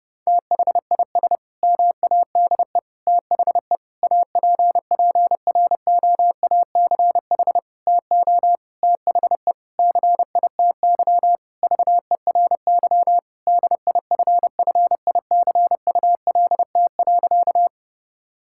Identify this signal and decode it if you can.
Morse code, heard as a keyed tone, about 30 wpm, 700 Hz. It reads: THIS MADE THE APPROACH TO THE CITY VERY DIFFICULT.
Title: Sentences from Top 1000 Words - Review Part 6 - 30wpm